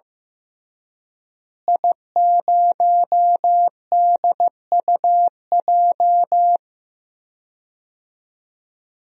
Morse code, heard as a keyed tone, about 15 wpm, 700 Hz. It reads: I0DUJ
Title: DXCC-100 Call Signs - 5 Characters - Repeated 2x - 15wpm